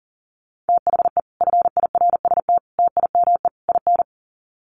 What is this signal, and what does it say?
Morse code, heard as a keyed tone, about 40 wpm, 700 Hz. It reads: THE FIRST TIME IN
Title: Sets of 4 Words 40wpm